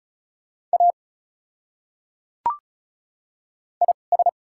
Morse code, heard as a keyed tone, about 36 wpm, spaced as for 5 wpm, 700 Hz. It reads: AIS
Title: 2x Character Spacing - Top 100 Words - 35wpm